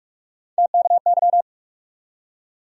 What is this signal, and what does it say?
Morse code, heard as a keyed tone, about 41 wpm, 700 Hz. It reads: TKY
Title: USA Call Sign Postfixes - 3 Letters 45wpm